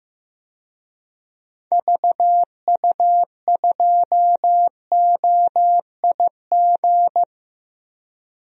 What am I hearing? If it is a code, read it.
Morse code, heard as a keyed tone, about 15 wpm, 700 Hz. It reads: VU2OIG
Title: DXCC-100 Call Signs - 6 Characters - Repeated 2x - 15wpm